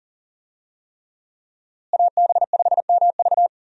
Morse code, heard as a keyed tone, about 38 wpm, 700 Hz. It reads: AB5MV
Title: USA Call Signs - 5 Characters - Repeated 3x - 40wpm